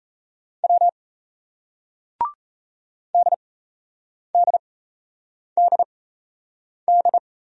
Morse code, heard as a keyed tone, about 38 wpm, spaced as for 8 wpm, 700 Hz. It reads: W DDDD